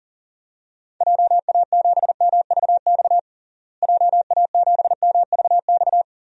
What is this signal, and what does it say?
Morse code, heard as a keyed tone, about 37 wpm, 700 Hz. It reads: JA7MVX JA7MVX
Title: DXCC-100 Call Signs - 6 Characters - Repeated 3x - 40wpm